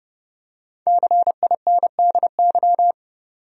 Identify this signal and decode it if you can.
Morse code, heard as a keyed tone, about 30 wpm, 700 Hz. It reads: CINDY
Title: USA Common Names - 30wpm